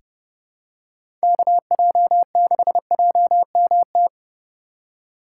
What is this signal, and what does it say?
Morse code, heard as a keyed tone, about 30 wpm, 700 Hz. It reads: KJ6JMT
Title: USA Call Signs - 6 Characters - 6x Character Spacing - 30wpm